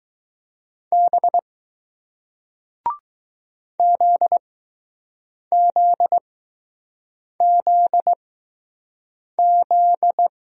Morse code, heard as a keyed tone, about 20 wpm, spaced as for 6 wpm, 700 Hz. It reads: B ZZZZ